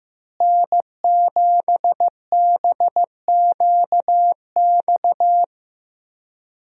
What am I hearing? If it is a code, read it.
Morse code, heard as a keyed tone, about 16 wpm, 700 Hz. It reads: N7BQX
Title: USA Call Signs - 5 Characters - Repeated 3x - 15wpm